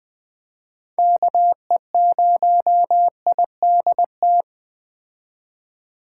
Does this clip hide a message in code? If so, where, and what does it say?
Yes: Morse code from 1.0 to 4.4 s, heard as a keyed tone, about 20 wpm, 700 Hz. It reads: KE0IDT